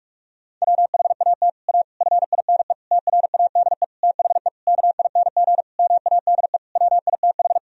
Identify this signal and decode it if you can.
Morse code, heard as a keyed tone, about 41 wpm, 700 Hz. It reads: WHAT A FINE TRADE THE KING MADE WITH
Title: Sentences from Top 1000 Words - Review Part 4 - 45wpm